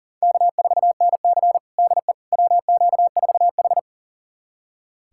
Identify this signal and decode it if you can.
Morse code, heard as a keyed tone, about 40 wpm, 700 Hz. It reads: K4NC DE WQ4H
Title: USA Callsign DE Callsign - 4 Characters - 40wpm